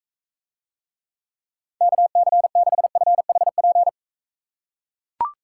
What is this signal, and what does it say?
Morse code, heard as a keyed tone, about 42 wpm, 700 Hz. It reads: KC6FHP